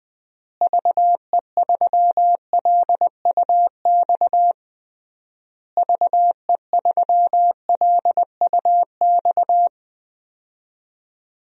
Morse code, heard as a keyed tone, about 20 wpm, 700 Hz. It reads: VE3LUX VE3LUX
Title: DXCC-100 Call Signs - 6 Characters - Repeated 3x - 20wpm